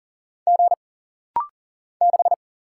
Morse code, heard as a keyed tone, about 40 wpm, 700 Hz. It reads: G 6
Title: Single Letter-Number - Rapid-Fire 40wpm